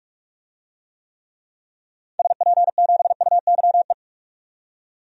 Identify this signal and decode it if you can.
Morse code, heard as a keyed tone, about 46 wpm, 700 Hz. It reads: SP7UYE
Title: DXCC-100 Call Signs - 6 Characters - 45wpm